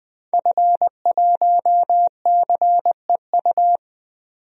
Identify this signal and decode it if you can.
Morse code, heard as a keyed tone, about 20 wpm, 700 Hz. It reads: F1CEU
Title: DXCC-100 Call Signs - 5 Characters - 20wpm